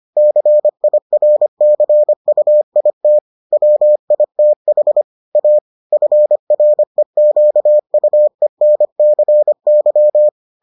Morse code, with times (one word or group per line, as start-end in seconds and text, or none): 0.2-3.2 CIRCUIT
3.5-5.0 WITH
5.4-5.6 A
5.9-10.3 FREQUENCY